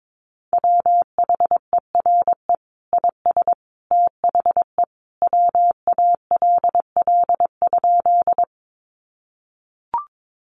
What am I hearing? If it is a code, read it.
Morse code, heard as a keyed tone, about 22 wpm, 700 Hz. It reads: WHERE IS THE WALL?
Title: Sentences from Top 1100 Words 22wpm